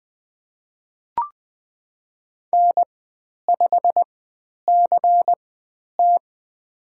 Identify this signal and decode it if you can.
Morse code, heard as a keyed tone, about 20 wpm, spaced as for 10 wpm, 700 Hz. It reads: N5CT
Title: 095 Lesson 16g - Letter c - Call Signs 20wpm